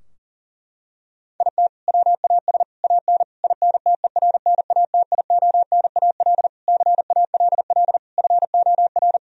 Morse code, heard as a keyed tone, about 40 wpm, 700 Hz. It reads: IT WAS AN INTERNATIONAL CALL FOR
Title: Sentences from Top 1000 Words - Review Part 4 - 40wpm